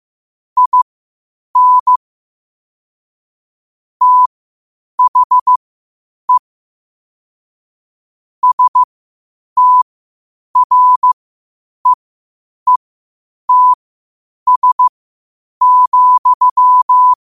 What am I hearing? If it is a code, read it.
Morse code, heard as a keyed tone, about 15 wpm, spaced as for 8 wpm, 1000 Hz. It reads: IN THE STREETS,
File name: qotdfiles / 8wpm